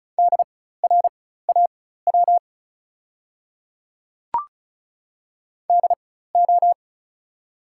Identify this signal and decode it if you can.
Morse code, heard as a keyed tone, about 36 wpm, spaced as for 16 wpm, 690 Hz. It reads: DRAW DO
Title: 4x Character Spacing - Top 300 Words - 35wpm